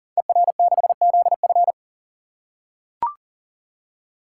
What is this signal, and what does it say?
Morse code, heard as a keyed tone, about 42 wpm, 700 Hz. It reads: ER6ZF
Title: Code Groups - 5 Letter:Numbers - 40wpm